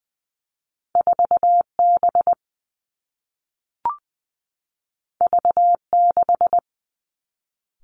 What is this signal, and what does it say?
Morse code, heard as a keyed tone, about 20 wpm, 700 Hz. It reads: 4 B V 6
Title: Two Challenging Letter-Number Combinations 20wpm